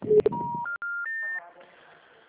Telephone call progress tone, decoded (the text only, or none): special information tone